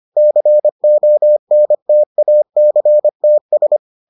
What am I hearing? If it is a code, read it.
Morse code, heard as a keyed tone, about 25 wpm, 600 Hz. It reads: CONTACTS